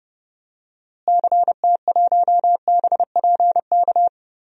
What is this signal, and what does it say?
Morse code, heard as a keyed tone, about 30 wpm, 700 Hz. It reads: CT1BPK
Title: DXCC-100 Call Signs - 6 Characters - Repeated 3x - 30wpm